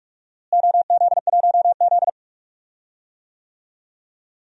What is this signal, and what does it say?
Morse code, heard as a keyed tone, about 42 wpm, 700 Hz. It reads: OZ1Z
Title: DXCC-100 Call Signs - 4 Characters - Repeated 2x - 45wpm